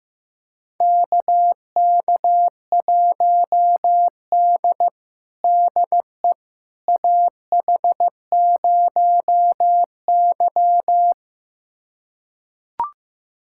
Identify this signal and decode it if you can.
Morse code, heard as a keyed tone, about 16 wpm, 700 Hz. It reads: KK1D DE AH0Y
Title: USA Callsign DE Callsign - 4 Characters - 15wpm